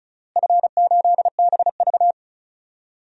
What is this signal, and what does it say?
Morse code, heard as a keyed tone, about 36 wpm, 700 Hz. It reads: F8BV
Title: DXCC-100 Call Signs - 4 Characters - Repeated 3x - 35wpm